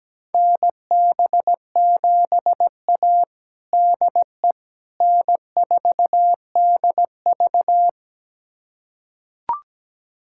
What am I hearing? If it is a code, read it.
Morse code, heard as a keyed tone, about 17 wpm, 700 Hz. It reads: NB7A DE N4DV
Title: USA Callsign DE Callsign - 4 Characters - 17wpm